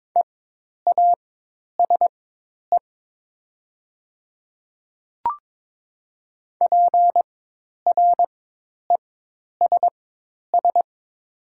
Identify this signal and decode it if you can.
Morse code, heard as a keyed tone, about 22 wpm, spaced as for 10 wpm, 700 Hz. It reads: EASE PRESS
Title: 4x Character Spacing - Top 500 Words - 22wpm